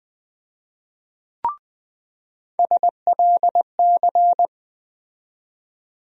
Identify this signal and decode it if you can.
Morse code, heard as a keyed tone, about 21 wpm, 700 Hz. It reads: SLC
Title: USA Call Sign Postfixes - 3 Letters 20wpm